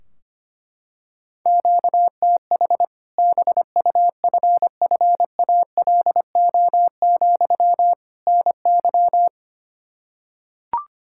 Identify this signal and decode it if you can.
Morse code, heard as a keyed tone, about 25 wpm, 700 Hz. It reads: QTH BUFFALO, NY
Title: Practice QSO - Exchange 1 - Beginning - 25wpm